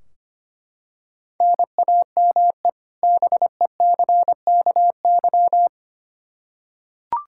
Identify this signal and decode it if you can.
Morse code, heard as a keyed tone, about 25 wpm, 700 Hz. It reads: NAME BECKY
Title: K1USN SST QSO Practice 25wpm